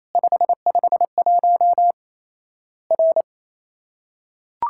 Morse code, heard as a keyed tone, about 28 wpm, 700 Hz. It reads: 551 R
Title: Worked All Europe DX Contest - QTC Exchange - 28wpm